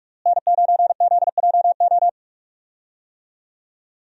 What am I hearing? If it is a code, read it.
Morse code, heard as a keyed tone, about 43 wpm, 700 Hz. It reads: N9ZJO
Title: USA Call Signs - 5 Characters - Repeated 3x - 45wpm